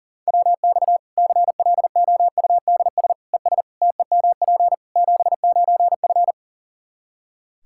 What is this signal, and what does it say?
Morse code, heard as a keyed tone, about 40 wpm, 700 Hz. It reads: WX CLOUDS ES TEMP 79F
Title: Practice QSO - Exchange 2 - Info - 40wpm